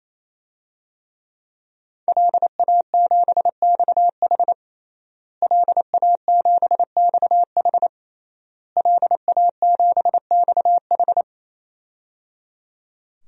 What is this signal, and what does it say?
Morse code, heard as a keyed tone, about 28 wpm, 700 Hz. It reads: LA7XH LA7XH LA7XH
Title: DXCC-100 Call Signs - 5 Characters - Repeated 3x - 28wpm